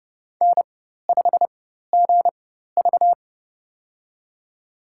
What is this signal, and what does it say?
Morse code, heard as a keyed tone, about 30 wpm, spaced as for 14 wpm, 700 Hz. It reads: N5GV
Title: USA Call Signs - 4x Character Spacing - 30wpm